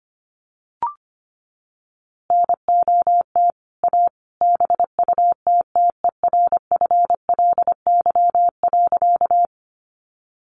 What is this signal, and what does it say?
Morse code, heard as a keyed tone, about 25 wpm, 700 Hz. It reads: NOT A BUTTERFLY.
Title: A Sound Of Thunder 25wpm